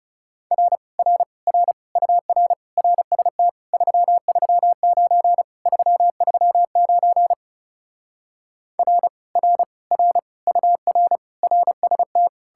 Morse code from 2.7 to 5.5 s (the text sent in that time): RST 339